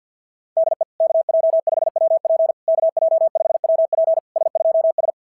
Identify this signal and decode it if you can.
Morse code, heard as a keyed tone, about 52 wpm, 690 Hz. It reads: DE KJ5WP KJ5WP S2S
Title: SOTA QSO Practice 50wpm